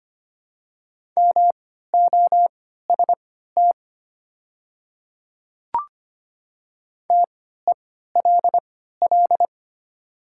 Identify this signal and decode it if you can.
Morse code, heard as a keyed tone, about 25 wpm, spaced as for 14 wpm, 700 Hz. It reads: MOST TELL
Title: 3x Character Spacing - Top 300 Words - 25wpm